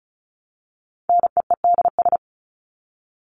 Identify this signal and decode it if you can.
Morse code, heard as a keyed tone, about 34 wpm, 700 Hz. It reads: NEEDS
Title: ES:S Suffix - Top 100 Words with Base Words 35wpm